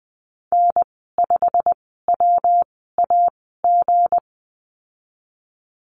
Morse code, heard as a keyed tone, about 20 wpm, spaced as for 15 wpm, 700 Hz. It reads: N5WAG
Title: 186 Lesson 30g - Letter b - Call Signs 20wpm